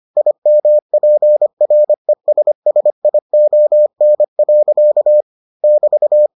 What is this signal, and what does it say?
Morse code, heard as a keyed tone, about 25 wpm, 600 Hz. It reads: IMPRESSION. <BT>